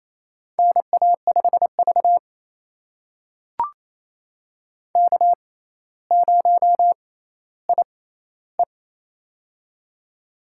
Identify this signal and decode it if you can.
Morse code, heard as a keyed tone, about 28 wpm, spaced as for 14 wpm, 700 Hz. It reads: NA5V K0IE
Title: USA Call Signs - 6x Character Spacing - 28wpm